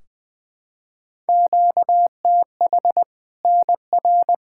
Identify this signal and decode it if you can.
Morse code, heard as a keyed tone, about 20 wpm, 700 Hz. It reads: QTH NR
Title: Practice QSO - Exchange 1 - Beginning - 20wpm